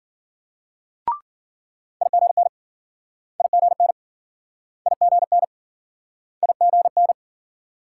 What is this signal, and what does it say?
Morse code, heard as a keyed tone, about 52 wpm, 700 Hz. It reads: IGN IGN IGN IGN